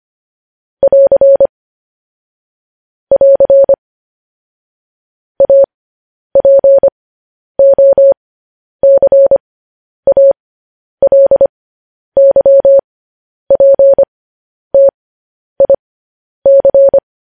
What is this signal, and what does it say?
Morse code, heard as a keyed tone, about 25 wpm, spaced as for 10 wpm, 550 Hz. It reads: <AR> <AR> APOCALYPTIC